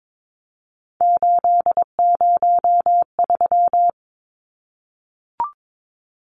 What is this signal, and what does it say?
Morse code, heard as a keyed tone, about 22 wpm, 700 Hz. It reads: 803
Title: Numbers - 3-Digit 22wpm